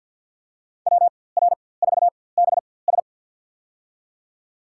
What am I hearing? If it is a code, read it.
Morse code, heard as a keyed tone, about 47 wpm, spaced as for 23 wpm, 700 Hz. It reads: WR4BS
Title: USA Call Signs - 5 Characters - 4x Character Spacing - 50wpm